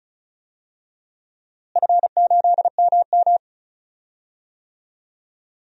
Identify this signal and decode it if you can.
Morse code, heard as a keyed tone, about 36 wpm, 700 Hz. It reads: F8MM
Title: DXCC-100 Call Signs - 4 Characters - Repeated 3x - 35wpm